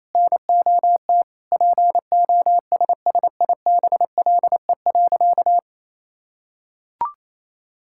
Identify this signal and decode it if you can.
Morse code, heard as a keyed tone, about 27 wpm, 700 Hz. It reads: NOT POSSIBLE.
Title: Sentences from Top 1000 Words - Review Part 6 - 28wpm